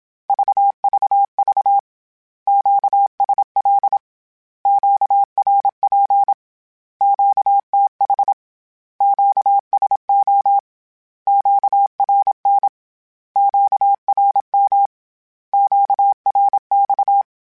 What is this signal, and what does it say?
Morse code, heard as a keyed tone, about 27 wpm, 800 Hz. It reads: VVV QSL QRP QTH QSO QRN QRM QRX
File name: Q Calls / Q Calls-mp3s / Q Calls Once @25